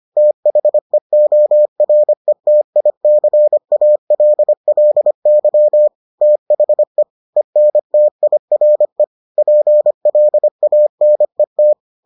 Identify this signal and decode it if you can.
Morse code, heard as a keyed tone, about 25 wpm, 600 Hz. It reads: THEORETICALLY THE ENTIRE PLANET